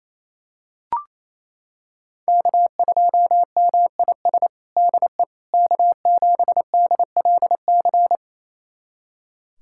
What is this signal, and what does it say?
Morse code, heard as a keyed tone, about 27 wpm, 700 Hz. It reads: K2MIS DE K7DLC
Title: USA Callsign DE Callsign - 5 Characters - 28wpm